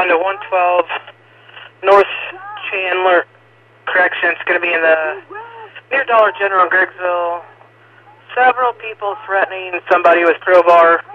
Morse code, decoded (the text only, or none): none